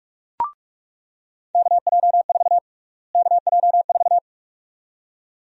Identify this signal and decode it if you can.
Morse code, heard as a keyed tone, about 44 wpm, 700 Hz. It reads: KJ4 KJ4
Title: USA Call Sign Prefixes - Repeated 2x - 45wpm